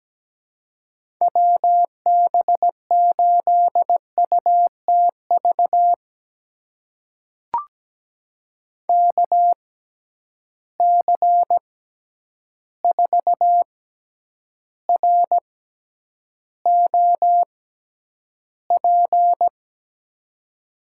Morse code, heard as a keyed tone, about 17 wpm, spaced as for 9 wpm, 700 Hz. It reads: WB8UTV KC4ROP
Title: USA Call Signs - 6 Characters - 6x Character Spacing - 17wpm